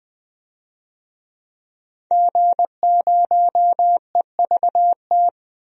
Morse code, heard as a keyed tone, about 20 wpm, 700 Hz. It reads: G0EVT
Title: DXCC-100 Call Signs - 5 Characters - Repeated 3x - 20wpm